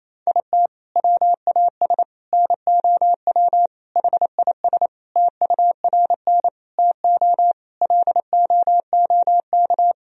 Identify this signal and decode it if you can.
Morse code, heard as a keyed tone, about 28 wpm, 700 Hz. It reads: IT WAS NOW HIS TURN TO LOOK
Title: Sentences from Top 200 Words and Abbreviations 28wpm